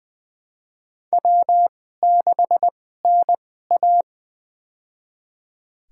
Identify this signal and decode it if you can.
Morse code, heard as a keyed tone, about 20 wpm, spaced as for 15 wpm, 700 Hz. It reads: W6NA
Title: USA Call Signs - 4 Characters - 2x Character Spacing - 20wpm